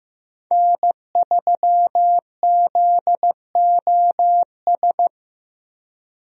Morse code, heard as a keyed tone, about 15 wpm, 700 Hz. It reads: N3ZOS
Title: USA Call Signs - 5 Characters - Repeated 2x - 15wpm